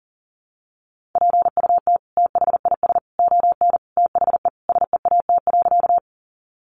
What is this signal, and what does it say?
Morse code, heard as a keyed tone, about 37 wpm, 700 Hz. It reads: PUT THIS ON THE SEAT.
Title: Sentences from Top 1500 Words 40wpm